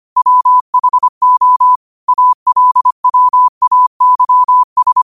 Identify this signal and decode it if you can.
Morse code, heard as a keyed tone, about 25 wpm, 1000 Hz. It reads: WHO ALWAYS